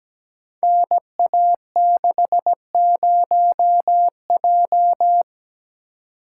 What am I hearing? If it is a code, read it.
Morse code, heard as a keyed tone, about 17 wpm, 700 Hz. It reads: NA60J
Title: Code Groups - 5 Letter:Numbers - 17wpm